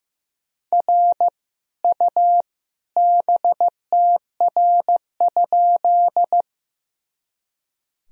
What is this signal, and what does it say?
Morse code, heard as a keyed tone, about 15 wpm, 700 Hz. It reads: R U BTR?
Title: Sentences from Top 400 Words and Abbreviations 15wpm